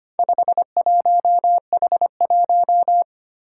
Morse code, heard as a keyed tone, about 25 wpm, 700 Hz. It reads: 51H1
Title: Four Challenging Letter-Number Combinations 25wpm